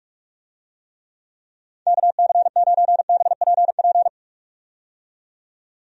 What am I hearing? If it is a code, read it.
Morse code, heard as a keyed tone, about 44 wpm, 700 Hz. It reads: KC9BPP
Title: USA Call Signs - 6 Characters - 45wpm